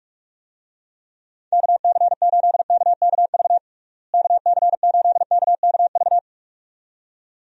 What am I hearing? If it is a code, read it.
Morse code, heard as a keyed tone, about 46 wpm, 700 Hz. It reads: KC8KKV KC8KKV
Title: USA Call Signs - 6 Characters - Repeated 2x - 45wpm